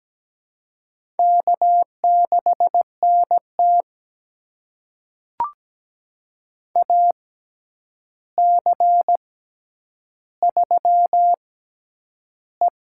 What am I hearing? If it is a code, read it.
Morse code, heard as a keyed tone, about 17 wpm, spaced as for 9 wpm, 700 Hz. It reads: K6NT AC3E
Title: USA Call Signs - 4 Characters - 6x Character Spacing - 17wpm